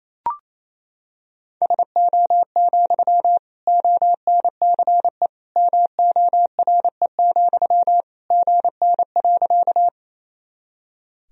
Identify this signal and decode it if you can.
Morse code, heard as a keyed tone, about 28 wpm, 700 Hz. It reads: SO, ONCE MORE, GN.
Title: Sentences from Top 300 Words and Abbreviations 28wpm